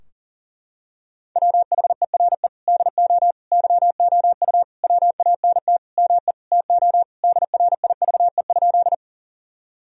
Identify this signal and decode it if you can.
Morse code, heard as a keyed tone, about 40 wpm, 700 Hz. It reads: WHERE DO YOU WANT ME TO DRIVE?
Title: Sentences from Top 1000 Words - Review Part 2 - 40wpm